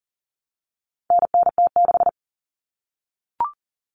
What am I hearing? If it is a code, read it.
Morse code, heard as a keyed tone, about 40 wpm, 700 Hz. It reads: NNT6
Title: Common Cut Numbers - 4-Digit 40wpm